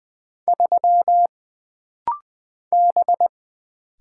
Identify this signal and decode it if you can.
Morse code, heard as a keyed tone, about 20 wpm, 700 Hz. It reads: 3 B
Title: Single Character (Letter:Number:Punctuation:Prosign) - Rapid-Fire - 20wpm